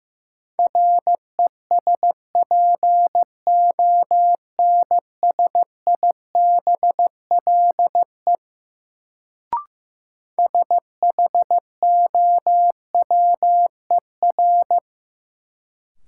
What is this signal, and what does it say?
Morse code, heard as a keyed tone, about 15 wpm, 700 Hz. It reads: RESPONSIBLE SHOWER
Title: Sentences from Top 1600 Words - New Word Drill Alt - 15wpm